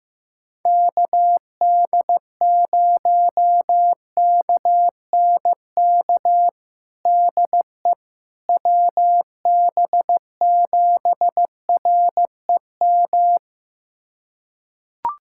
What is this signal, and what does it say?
Morse code, heard as a keyed tone, about 15 wpm, 700 Hz. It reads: KD0KNK DE WB7REM
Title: USA Callsign DE Callsign - 6 Characters - 15wpm